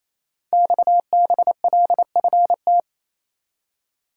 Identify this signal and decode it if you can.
Morse code, heard as a keyed tone, about 28 wpm, 700 Hz. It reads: XBLFT
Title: Code Groups - 5 Letters - 28wpm